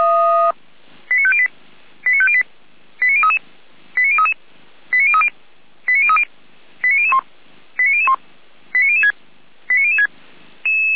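Five-tone selective call (ZVEI), heard as quick runs of five tones, tones of about 70 ms. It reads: 89508 89508 89033 89033 89030 89030 89001 89001 89006 89006
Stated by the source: ZVEI five-tone call 89099 received by SDR, from the feuerwehr-orlamuende.org server (fire brigade Orlamünde).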